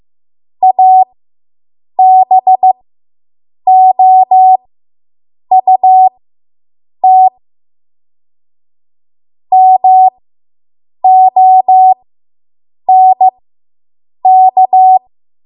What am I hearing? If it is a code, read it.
Morse code, heard as a keyed tone, about 15 wpm, spaced as for 7 wpm, 750 Hz. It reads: ABOUT MONK